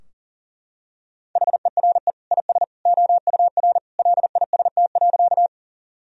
Morse code, heard as a keyed tone, about 40 wpm, 700 Hz. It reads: HERE IS OUR LIST.
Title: Sentences from Top 1000 Words - Review Part 2 - 40wpm